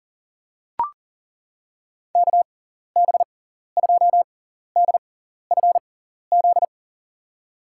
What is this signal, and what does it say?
Morse code, heard as a keyed tone, about 40 wpm, spaced as for 14 wpm, 700 Hz. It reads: KB2DFZ
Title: USA Call Signs - 6x Character Spacing - 40wpm